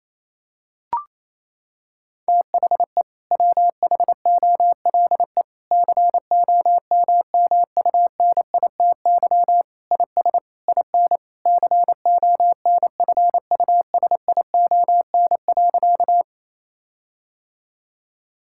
Morse code, heard as a keyed tone, about 28 wpm, 700 Hz. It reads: THE WHOLE COMMUNITY IS IN CONFUSION.
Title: Sentences from Top 1600 Words 28wpm